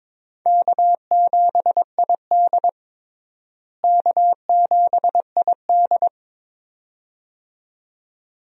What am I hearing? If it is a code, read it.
Morse code, heard as a keyed tone, about 22 wpm, 700 Hz. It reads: K7ID K7ID
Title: USA Call Signs - 4 Characters - Repeated 2x - 22wpm